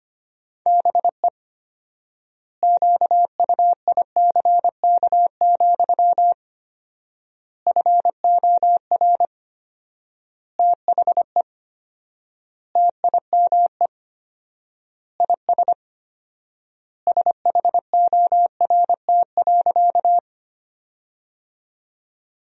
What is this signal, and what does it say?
Morse code, heard as a keyed tone, about 25 wpm, 700 Hz. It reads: BE QUICK, FOR THE TIME IS SHORT.
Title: 4x Word-Spacing - Sentences from Top 500 Words - 25wpm